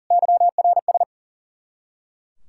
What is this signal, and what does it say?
Morse code, heard as a keyed tone, about 49 wpm, 700 Hz. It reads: YRS
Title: Speed-Racing Common QSO Phrases - 40wpm